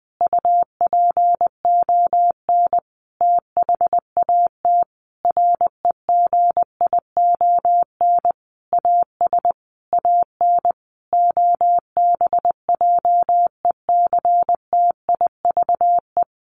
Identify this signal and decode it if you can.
Morse code, heard as a keyed tone, about 20 wpm, 700 Hz. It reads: UPON THAT REGION AS AN OBJECTIVE
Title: Sentences from Top 1500 Words 20wpm